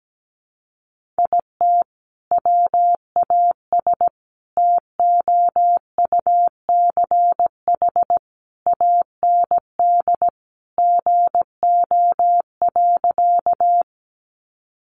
Morse code, heard as a keyed tone, about 17 wpm, 700 Hz. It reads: IT WAS TOUCH AND GO.
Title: Sentences from Top 1200 Words 17wpm